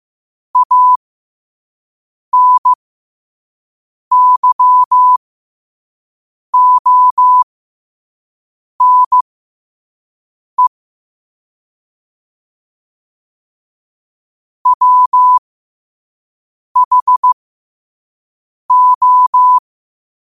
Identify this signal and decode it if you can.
Morse code, heard as a keyed tone, about 15 wpm, spaced as for 5 wpm, 1000 Hz. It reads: ANYONE WHO
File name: qotdfiles / 5wpm